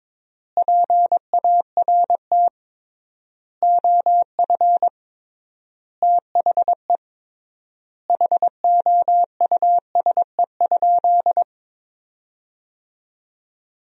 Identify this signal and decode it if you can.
Morse code, heard as a keyed tone, about 22 wpm, 700 Hz. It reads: PART OF THE HOUSE?
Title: 3x Word-Spacing - Sentences from Top 200 Words - 22wpm